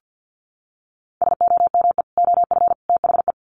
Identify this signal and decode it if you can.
Morse code, heard as a keyed tone, about 50 wpm, 700 Hz. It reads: SOME OF THE
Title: Sets of 3 Words 50wpm